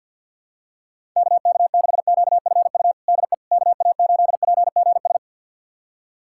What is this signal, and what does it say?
Morse code, heard as a keyed tone, about 50 wpm, 700 Hz. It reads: KK6QFU DE KA8PGS
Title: USA Callsign DE Callsign - 6 Characters - 50wpm